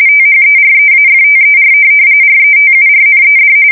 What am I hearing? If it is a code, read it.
RTTY, about 45.45 baud. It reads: THEREAFTER RETURNED
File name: fsk170 / USS-Missouri